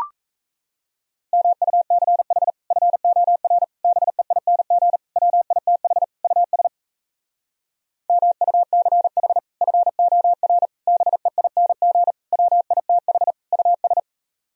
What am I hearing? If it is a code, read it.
Morse code, heard as a keyed tone, about 38 wpm, 700 Hz. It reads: MUCH FOR BEING WITH US MUCH FOR BEING WITH US